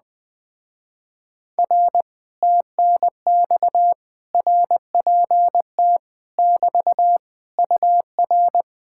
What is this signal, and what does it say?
Morse code, heard as a keyed tone, about 20 wpm, 700 Hz. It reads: R TNX RPT <BT> UR
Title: Speed-Racing Common QSO Phrases Encore - 17wpm